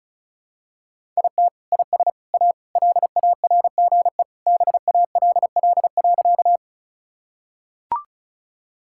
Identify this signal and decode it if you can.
Morse code, heard as a keyed tone, about 36 wpm, 700 Hz. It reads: IT IS A LARGE BALL.
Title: Sentences from Top 1000 Words - Review Part 6 - 35wpm